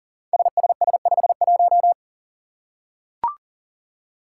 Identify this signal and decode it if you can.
Morse code, heard as a keyed tone, about 42 wpm, 700 Hz. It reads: SSS51
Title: Five Challenging Letter-Number Combinations 40wpm